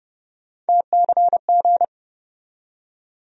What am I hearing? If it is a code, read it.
Morse code, heard as a keyed tone, about 30 wpm, 700 Hz. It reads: TCG
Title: USA Call Sign Postfixes - 3 Letters 30wpm